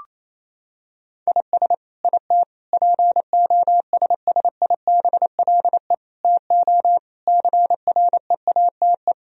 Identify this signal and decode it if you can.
Morse code, heard as a keyed tone, about 28 wpm, 700 Hz. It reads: IS IT POSSIBLE TO CREATE
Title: Sentences from Top 1000 Words - Review Part 3 - 28wpm